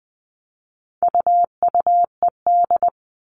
Morse code, heard as a keyed tone, about 20 wpm, 700 Hz. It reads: UUED
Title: All Cut Numbers - 4-Digit 20wpm